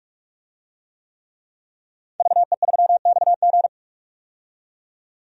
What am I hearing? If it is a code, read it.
Morse code, heard as a keyed tone, about 45 wpm, 700 Hz. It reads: VE3XG
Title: DXCC-100 Call Signs - 5 Characters - 45wpm